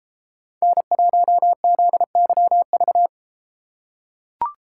Morse code, heard as a keyed tone, about 31 wpm, 700 Hz. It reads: N1ZYV